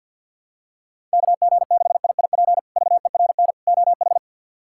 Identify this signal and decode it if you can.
Morse code, heard as a keyed tone, about 53 wpm, 700 Hz. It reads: KG6IIP VERN OH
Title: K1USN SST QSO Practice - 50wpm